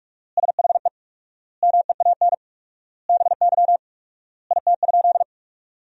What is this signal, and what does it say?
Morse code, heard as a keyed tone, about 46 wpm, 700 Hz. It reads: SHE MEAN BY IT?
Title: 4x Word-Spacing - Sentences from Top 200 Words - 45wpm